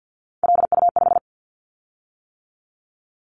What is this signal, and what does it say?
Morse code, heard as a keyed tone, about 50 wpm, 700 Hz. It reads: LU5
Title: Callsign Prefixes for Top 100 DXCC - 50wpm